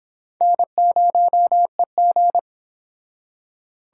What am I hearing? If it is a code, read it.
Morse code, heard as a keyed tone, about 26 wpm, 700 Hz. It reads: N0EG
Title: Speed Racing - USA 1x2 Call Signs 22wpm